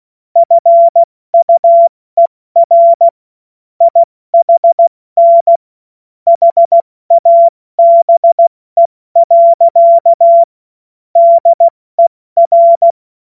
Morse code, heard as a keyed tone, about 16 wpm, 680 Hz. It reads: FUER IHN HABE. DER
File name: Witz-16-September-2025-70-F80